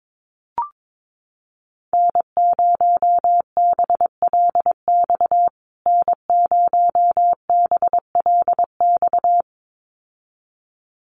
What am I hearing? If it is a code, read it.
Morse code, heard as a keyed tone, about 22 wpm, 700 Hz. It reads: N0BLX N0BLX
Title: USA Call Signs Repeated 2x 22wpm